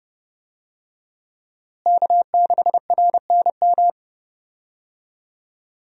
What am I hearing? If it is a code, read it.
Morse code, heard as a keyed tone, about 30 wpm, 700 Hz. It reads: K6RNM
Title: USA Call Signs - 5 Characters - Repeated 3x - 30wpm